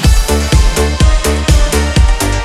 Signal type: none